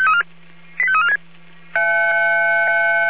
Five-tone selective call (ZVEI), heard as quick runs of five tones, tones of about 70 ms. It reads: 85346
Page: SDR/ZVEI